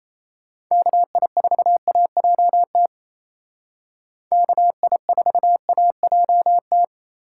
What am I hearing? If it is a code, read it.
Morse code, heard as a keyed tone, about 34 wpm, 700 Hz. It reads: KI4AJT KI4AJT
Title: Speed Racing - USA Call Signs 28wpm